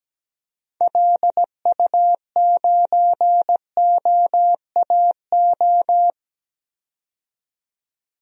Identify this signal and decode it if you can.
Morse code, heard as a keyed tone, about 17 wpm, 700 Hz. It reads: LU9OAO
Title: DXCC-100 Call Signs - 6 Characters - 17wpm